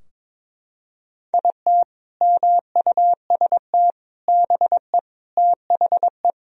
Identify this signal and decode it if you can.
Morse code, heard as a keyed tone, about 22 wpm, 700 Hz. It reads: IT MUST BE THE
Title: Sentences from Top 1000 Words - Review Part 4 - 22wpm